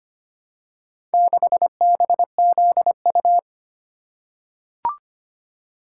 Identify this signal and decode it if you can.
Morse code, heard as a keyed tone, about 25 wpm, 700 Hz. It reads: 6BZU
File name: practice / 25wpm / Four Letter-Number Combinations - 25wpm